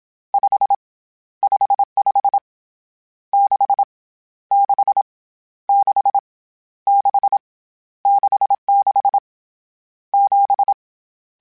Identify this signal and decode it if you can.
Morse code, heard as a keyed tone, about 27 wpm, 800 Hz. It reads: 5 55 6 6 6 6 66 7
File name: Misc / Misc-mp3s / Numb @25